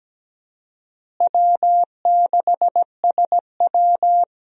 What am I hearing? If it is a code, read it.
Morse code, heard as a keyed tone, about 17 wpm, 700 Hz. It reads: W6SW
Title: USA Call Signs - 4 Characters - Repeated 2x - 17wpm